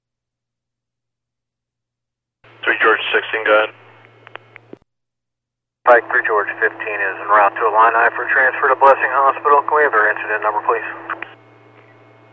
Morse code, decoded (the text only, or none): none